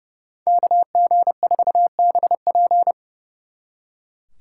Morse code, heard as a keyed tone, about 36 wpm, 700 Hz. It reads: KG4BP
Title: Speed Racing - USA 2x2 Call Signs 30wpm